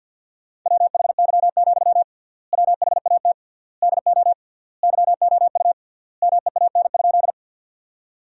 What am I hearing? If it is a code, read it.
Morse code, heard as a keyed tone, about 50 wpm, 700 Hz. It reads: WHY, WHAT DO YOU MEAN?
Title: 3x Word-Spacing - Sentences from Top 200 Words - 50wpm